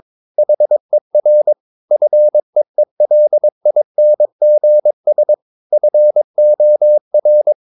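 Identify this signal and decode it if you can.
Morse code, heard as a keyed tone, about 22 wpm, 600 Hz. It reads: HER FEELINGS FOR
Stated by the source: Morse code practice at 22 wpm, sending text from an English novel.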